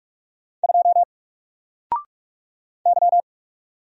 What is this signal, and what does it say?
Morse code, heard as a keyed tone, about 44 wpm, 700 Hz. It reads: 2 Y
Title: Single Character (Letter:Number:Punctuation:Prosign) Alt 45wpm